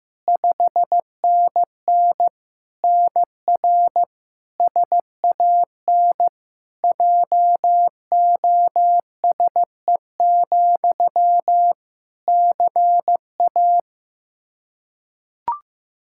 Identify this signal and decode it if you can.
Morse code, heard as a keyed tone, about 15 wpm, 700 Hz. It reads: 5NN NR SAN JOSE, CA
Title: Practice QSO - Exchange 1 - Beginning - 15wpm